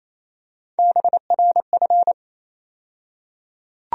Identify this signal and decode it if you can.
Morse code, heard as a keyed tone, about 27 wpm, 700 Hz. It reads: BRF
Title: USA Call Sign Postfixes - 3 Letters 28wpm